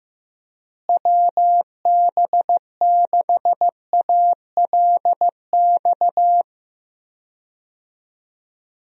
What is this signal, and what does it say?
Morse code, heard as a keyed tone, about 15 wpm, 700 Hz. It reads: WB6ALX
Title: USA Call Signs - 6 Characters - Repeated 3x - 15wpm